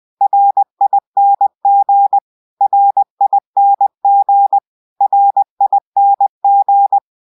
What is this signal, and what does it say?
Morse code, heard as a keyed tone, about 20 wpm, 800 Hz. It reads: RING RING RING